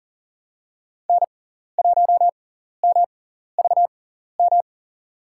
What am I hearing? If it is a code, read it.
Morse code, heard as a keyed tone, about 40 wpm, spaced as for 14 wpm, 700 Hz. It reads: N1MVM
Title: USA Call Signs - 5 Characters - 6x Character Spacing - 40wpm